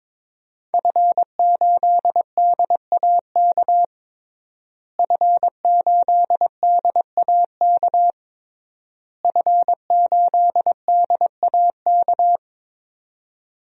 Morse code, heard as a keyed tone, about 22 wpm, 700 Hz. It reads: F8DAK F8DAK F8DAK
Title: DXCC-100 Call Signs - 5 Characters - Repeated 3x - 22wpm